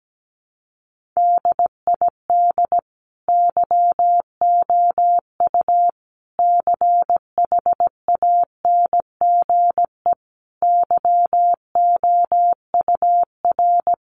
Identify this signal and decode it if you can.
Morse code, heard as a keyed tone, about 17 wpm, 700 Hz. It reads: DID YOU CHANGE YOUR
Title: Sentences from Top 1400 Words 17wpm